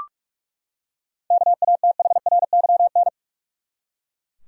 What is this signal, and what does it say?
Morse code, heard as a keyed tone, about 44 wpm, 700 Hz. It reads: KATHRYN
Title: USA Common Names - 45wpm